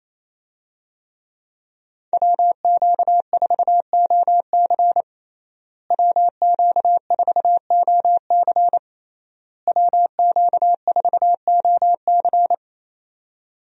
Morse code, heard as a keyed tone, about 28 wpm, 700 Hz. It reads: WQ4OC WQ4OC WQ4OC
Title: USA Call Signs - 5 Characters - Repeated 3x - 28wpm